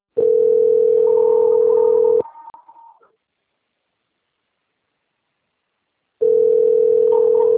Ringback tone (North America), 440 + 480 Hz.
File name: out- - - - - .7072